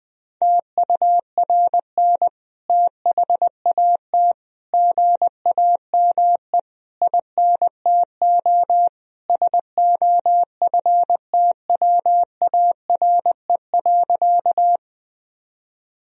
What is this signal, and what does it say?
Morse code, heard as a keyed tone, about 20 wpm, 700 Hz. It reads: TURN THAT GAME INTO SOFTWARE.
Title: Sentences from Top 1000 Words - Review Part 3 - 20wpm